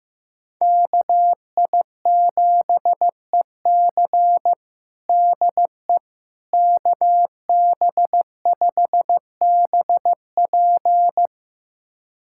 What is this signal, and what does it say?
Morse code, heard as a keyed tone, about 15 wpm, 700 Hz. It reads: KI7EC DE KB5BP
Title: USA Callsign DE Callsign - 5 Characters - 15wpm